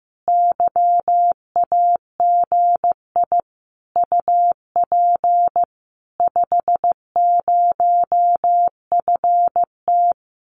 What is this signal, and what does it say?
Morse code, heard as a keyed tone, about 15 wpm, 700 Hz. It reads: YAGI UP 50FT
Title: Common QSO Phrases Encore 15wpm